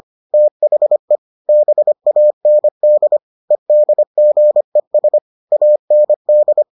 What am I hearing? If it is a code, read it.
Morse code, heard as a keyed tone, about 25 wpm, 600 Hz. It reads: THE BAND EDGES AND